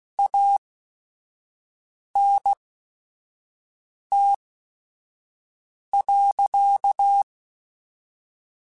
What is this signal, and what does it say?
Morse code, heard as a keyed tone, about 16 wpm, spaced as for 5 wpm, 800 Hz. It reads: ANT.
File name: QSOs / QSOs-mp3s / WD0FBT 1@5